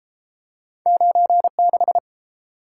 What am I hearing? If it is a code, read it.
Morse code, heard as a keyed tone, about 30 wpm, 700 Hz. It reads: 96